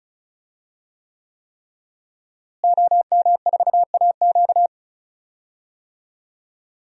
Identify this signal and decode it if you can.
Morse code, heard as a keyed tone, about 36 wpm, 700 Hz. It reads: OM4AQ
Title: DXCC-100 Call Signs - 5 Characters - Repeated 3x - 35wpm